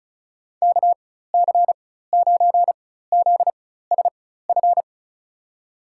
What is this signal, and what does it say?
Morse code, heard as a keyed tone, about 34 wpm, spaced as for 16 wpm, 700 Hz. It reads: KC9ZSF
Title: USA Call Signs - 6 Characters - 4x Character Spacing - 35wpm